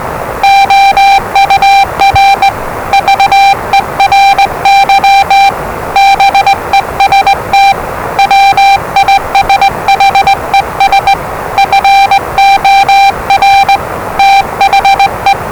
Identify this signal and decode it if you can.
Morse code, heard as a keyed tone, about 20 wpm, 790 Hz, with noise. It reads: OUR VERY BEST WISHES FOR THE
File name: WellingtonRadioZlwCloseDownOnHfAndMf / 01-Zlw 4277khz 417.5 qrt until 0925z